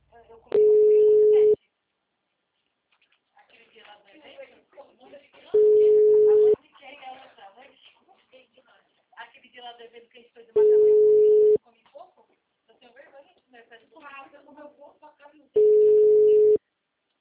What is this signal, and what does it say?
Ringback tone (Europe, 425 Hz), with speech.